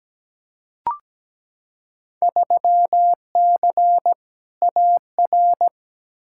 Morse code, heard as a keyed tone, about 17 wpm, 700 Hz. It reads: 3C AR
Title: Field Day Exchanges Encore - 17wpm